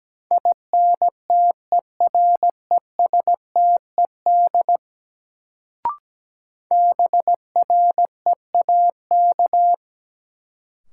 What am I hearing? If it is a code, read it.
Morse code, heard as a keyed tone, about 17 wpm, 700 Hz. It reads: INTERESTED BREAK
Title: Sentences from Top 900 Words - New Word Drill - 17wpm